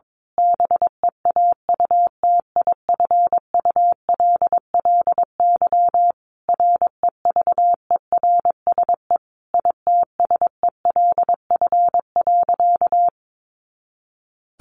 Morse code, heard as a keyed tone, about 22 wpm, 700 Hz. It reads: BEAUTIFULLY REVERSE ITSELF.
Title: A Sound Of Thunder 22wpm